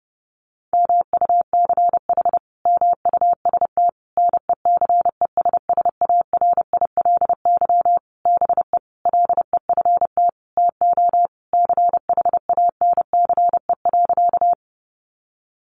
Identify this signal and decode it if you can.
Morse code, heard as a keyed tone, about 30 wpm, 700 Hz. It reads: MUCH MUST NECESSARILY BE LEFT TO CHANCE.
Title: Sentences from Top 1400 Words 30wpm